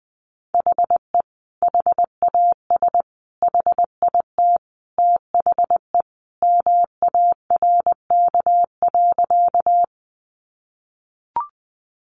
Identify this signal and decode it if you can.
Morse code, heard as a keyed tone, about 19 wpm, 700 Hz. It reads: HE HAS HIT THE MARK.
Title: Sentences from Top 1200 Words 20wpm